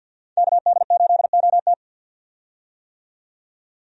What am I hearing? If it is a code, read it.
Morse code, heard as a keyed tone, about 48 wpm, 700 Hz. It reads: KD8OT
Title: USA Call Signs - 5 Characters - 50wpm